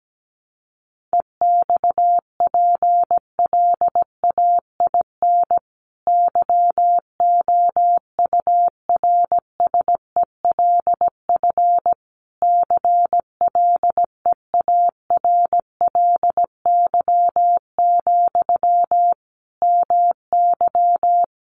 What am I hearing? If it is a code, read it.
Morse code, heard as a keyed tone, about 17 wpm, 700 Hz. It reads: EXPLAIN YOURSELF CLEARLY, MY
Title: Sentences from Top 1300 Words 17wpm